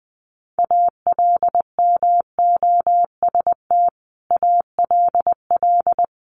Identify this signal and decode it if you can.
Morse code, heard as a keyed tone, about 20 wpm, 700 Hz. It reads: ALMOST ALL
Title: Sentences from Top 1500 Words 20wpm